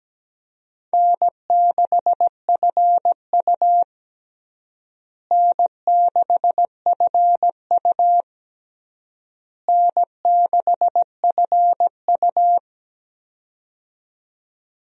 Morse code, heard as a keyed tone, about 17 wpm, 700 Hz. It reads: N6FU N6FU N6FU
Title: USA Call Signs - 4 Characters - Repeated 3x - 17wpm